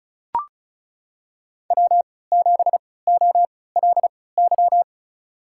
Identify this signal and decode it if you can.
Morse code, heard as a keyed tone, about 36 wpm, spaced as for 20 wpm, 700 Hz. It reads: W7OLY
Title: USA Call Signs - 3x Character Spacing - 35wpm